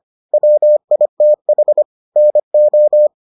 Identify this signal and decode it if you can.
Morse code, heard as a keyed tone, about 25 wpm, 600 Hz. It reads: WITH NO